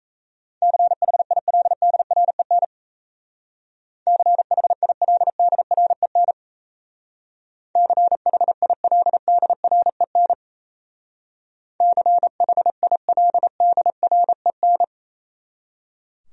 Morse code, heard as a keyed tone, about 35 wpm, 700 Hz. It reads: CHILDREN CHILDREN CHILDREN CHILDREN
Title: Speed-Racing Top 300 Words 28wpm